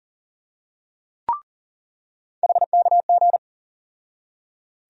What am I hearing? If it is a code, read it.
Morse code, heard as a keyed tone, about 40 wpm, 700 Hz. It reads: HKG
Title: USA Call Sign Postfixes - 3 Letters 40wpm